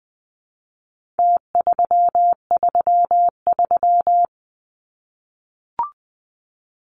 Morse code, heard as a keyed tone, about 20 wpm, 700 Hz. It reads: T333
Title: Numbers - Contest Style - 4-Digit 20wpm